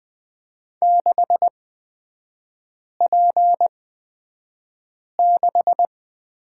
Morse code, heard as a keyed tone, about 20 wpm, spaced as for 5 wpm, 700 Hz. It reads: 6P6
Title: 236 Lesson 38e - Letter q - Three Characters 20wpm